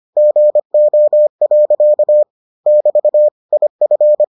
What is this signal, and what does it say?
Morse code, heard as a keyed tone, about 25 wpm, 600 Hz. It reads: GO. <BT> IF